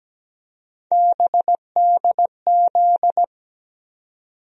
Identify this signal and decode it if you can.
Morse code, heard as a keyed tone, about 17 wpm, 700 Hz. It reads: BDZ
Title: USA Call Sign Postfixes - 3 Letters 17wpm